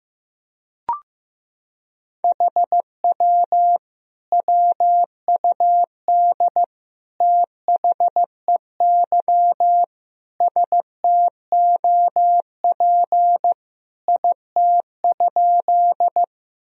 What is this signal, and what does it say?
Morse code, heard as a keyed tone, about 15 wpm, 700 Hz. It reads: HW WUD THEY STOP IT?
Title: Sentences from Top 300 Words and Abbreviations 15wpm